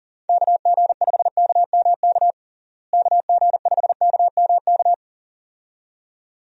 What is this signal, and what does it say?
Morse code, heard as a keyed tone, about 40 wpm, 700 Hz. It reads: KG5KMK KG5KMK
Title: USA Call Signs - 6 Characters - Repeated 2x - 40wpm